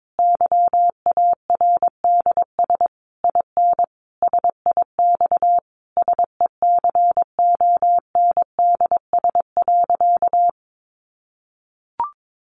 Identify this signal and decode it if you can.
Morse code, heard as a keyed tone, about 22 wpm, 700 Hz. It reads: YARDS IN SIX SECONDS.
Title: A Sound Of Thunder 22wpm